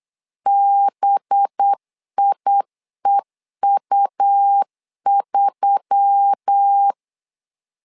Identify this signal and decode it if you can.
Morse code, heard as a keyed tone, about 8 wpm, 800 Hz. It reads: BIEU3
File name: LEZIONI / 5 LBVC / 5 LB 8wpm 2816